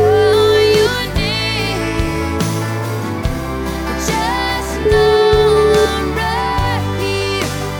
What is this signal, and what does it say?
Ringback tone (Europe, 425 Hz), with noise.